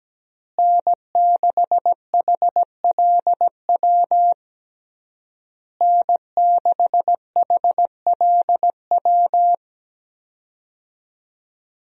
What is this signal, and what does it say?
Morse code, heard as a keyed tone, about 17 wpm, 700 Hz. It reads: N6HLW N6HLW
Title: USA Call Signs - 5 Characters - Repeated 3x - 17wpm